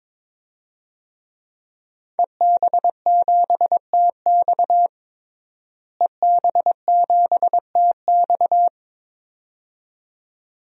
Morse code, heard as a keyed tone, about 22 wpm, 700 Hz. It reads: EB7TX EB7TX
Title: DXCC-100 Call Signs - 5 Characters - Repeated 2x - 22wpm